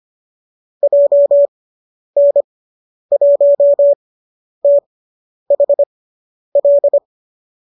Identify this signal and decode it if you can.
Morse code, heard as a keyed tone, about 25 wpm, spaced as for 10 wpm, 570 Hz. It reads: JN1THL